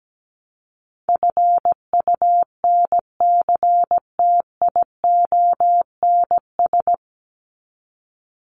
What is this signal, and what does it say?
Morse code, heard as a keyed tone, about 17 wpm, 700 Hz. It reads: FUNCTIONS
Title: ES:S Suffix - Top 100 Words with Base Words 17wpm